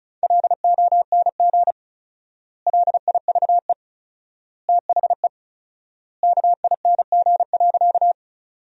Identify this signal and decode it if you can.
Morse code, heard as a keyed tone, about 36 wpm, 700 Hz. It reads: LONG LIVE THE KING.
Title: 4x Word-Spacing - Sentences from Top 400 Words - 35wpm